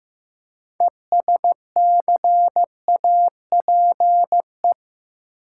Morse code, heard as a keyed tone, about 15 wpm, 700 Hz. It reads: ESCAPE
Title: Six Letter Words - 15wpm